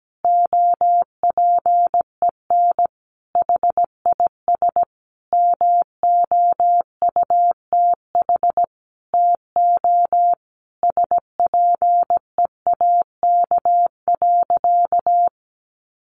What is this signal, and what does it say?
Morse code, heard as a keyed tone, about 17 wpm, 700 Hz. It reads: OPEN HIS MOUTH TO SPEAK.
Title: Sentences from Top 1500 Words 17wpm